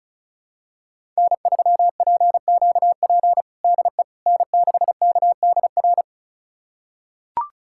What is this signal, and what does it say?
Morse code, heard as a keyed tone, about 36 wpm, 700 Hz. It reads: N3PQP DE N6KDR
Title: USA Callsign DE Callsign - 5 Characters - 35wpm